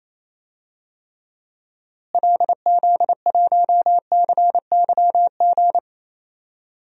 Morse code, heard as a keyed tone, about 28 wpm, 700 Hz. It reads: LZ1CYG
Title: DXCC-100 Call Signs - 6 Characters - Repeated 2x - 28wpm